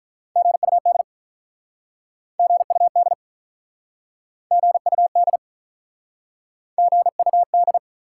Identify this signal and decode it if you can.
Morse code, heard as a keyed tone, about 48 wpm, 700 Hz. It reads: GUD GUD GUD GUD